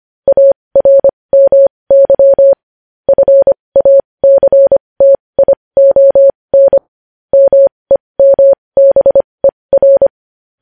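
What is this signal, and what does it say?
Morse code, heard as a keyed tone, about 25 wpm, spaced as for 20 wpm, 550 Hz. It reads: ARMY FACTION MEMBER
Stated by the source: Morse code news headlines sent at 20 wpm.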